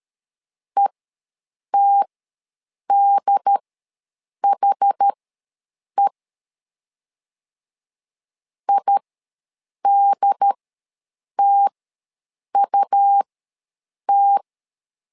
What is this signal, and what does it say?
Morse code, heard as a keyed tone, about 13 wpm, spaced as for 7 wpm, 800 Hz. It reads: ETDHE IDTUT
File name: LEZIONI / 2 UHDN / 2 UHD ws 12wpm 1722